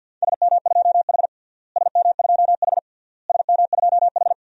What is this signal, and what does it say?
Morse code, heard as a keyed tone, about 48 wpm, 700 Hz. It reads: SM2H SM2H SM2H
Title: DXCC-100 Call Signs - 4 Characters - Repeated 3x - 50wpm